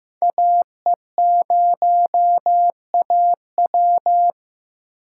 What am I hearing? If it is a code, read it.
Morse code, heard as a keyed tone, about 15 wpm, 700 Hz. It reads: AE0AW
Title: USA Call Signs - 5 Characters - 3x Character Spacing - 15wpm